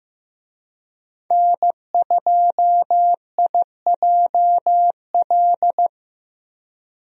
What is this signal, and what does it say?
Morse code, heard as a keyed tone, about 15 wpm, 700 Hz. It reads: N2IJL
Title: USA Call Signs - 5 Characters - 15wpm